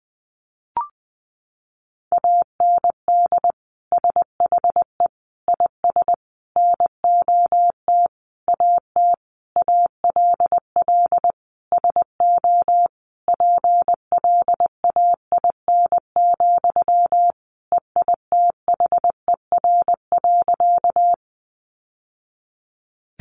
Morse code, heard as a keyed tone, about 20 wpm, 700 Hz. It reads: AND SHE IS NOT AT ALL SO PLAIN, EITHER.
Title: Sentences from Top 600 Words 20wpm